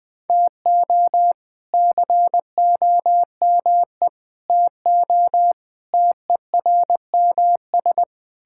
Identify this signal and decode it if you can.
Morse code, heard as a keyed tone, about 21 wpm, 700 Hz. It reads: TO COME TO TERMS
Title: Sentences with 8-Word Phrases - 20wpm